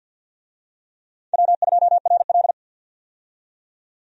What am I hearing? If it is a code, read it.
Morse code, heard as a keyed tone, about 48 wpm, 700 Hz. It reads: W2RL
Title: USA Call Signs - 4 Characters - 50wpm